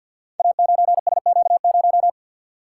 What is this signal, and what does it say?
Morse code, heard as a keyed tone, about 48 wpm, 700 Hz. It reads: A9SQ0
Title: Code Groups - 5 Letter:Numbers - 50wpm